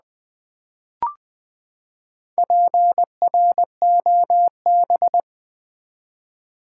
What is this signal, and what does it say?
Morse code, heard as a keyed tone, about 21 wpm, 700 Hz. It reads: PROB
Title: Build A Word - 4 Letters At A Time - 11-Letter Words - 20wpm